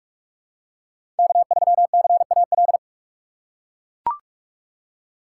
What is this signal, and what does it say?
Morse code, heard as a keyed tone, about 44 wpm, 700 Hz. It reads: K3CAL
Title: USA Call Signs - 5 Characters - 6x Character Spacing - 45wpm